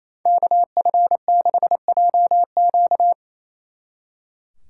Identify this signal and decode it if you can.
Morse code, heard as a keyed tone, about 34 wpm, 700 Hz. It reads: KF6JQ
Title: Speed Racing - USA 2x2 Call Signs 28wpm